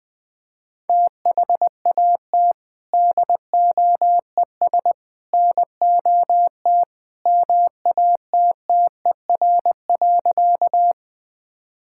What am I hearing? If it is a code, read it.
Morse code, heard as a keyed tone, about 20 wpm, 700 Hz. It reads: THAT DOES NOT MATTER.
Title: Sentences with 3-Word Phrases - Spoken Beginning - 20wpm